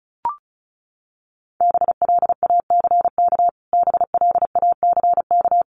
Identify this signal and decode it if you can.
Morse code, heard as a keyed tone, about 34 wpm, 700 Hz. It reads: BLACK BLACK
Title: Top 500 Words Repeated 35wpm